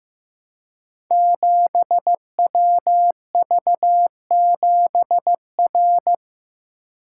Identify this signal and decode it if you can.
Morse code, heard as a keyed tone, about 15 wpm, 700 Hz. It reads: 7WV7R
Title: Code Groups - 5 Letter:Numbers - 15wpm